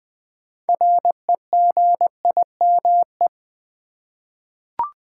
Morse code, heard as a keyed tone, about 20 wpm, 700 Hz. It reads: REGIME
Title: 6 Letter Words - First 3 Letters Given - 20wpm